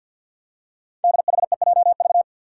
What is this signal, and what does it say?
Morse code, heard as a keyed tone, about 52 wpm, 700 Hz. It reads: DHEJV
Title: Five Challenging Letter-Number Combinations 50wpm